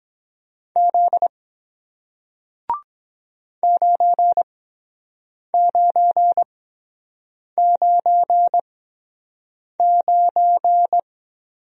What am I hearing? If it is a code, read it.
Morse code, heard as a keyed tone, about 23 wpm, spaced as for 7 wpm, 700 Hz. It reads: Z 9999